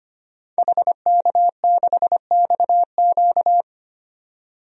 Morse code, heard as a keyed tone, about 25 wpm, 700 Hz. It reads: HK6XQ
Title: DXCC-100 Call Signs - 5 Characters - 25wpm